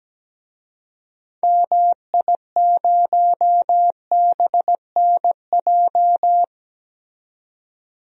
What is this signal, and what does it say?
Morse code, heard as a keyed tone, about 17 wpm, 700 Hz. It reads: MI0BNJ
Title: DXCC-100 Call Signs - 6 Characters - 17wpm